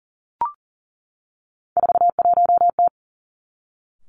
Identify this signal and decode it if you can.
Morse code, heard as a keyed tone, about 40 wpm, 700 Hz. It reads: 41T
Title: Three Challenging Letter-Number Combinations 40wpm